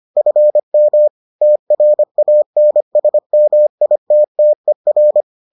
Morse code, heard as a keyed tone, about 25 wpm, 600 Hz. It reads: FM TRANSMITTER